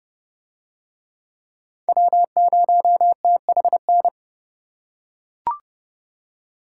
Morse code, heard as a keyed tone, about 31 wpm, 700 Hz. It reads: W0THN